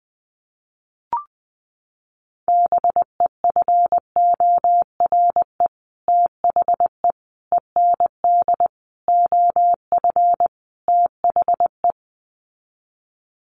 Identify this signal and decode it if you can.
Morse code, heard as a keyed tone, about 20 wpm, 700 Hz. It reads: BEFORE THE END OF THE
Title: Sets of 5 Words Encore 20wpm